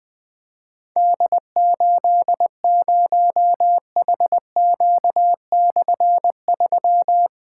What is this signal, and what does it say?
Morse code, heard as a keyed tone, about 21 wpm, 700 Hz. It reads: D80HQ/3
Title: Difficult Call Signs 20wpm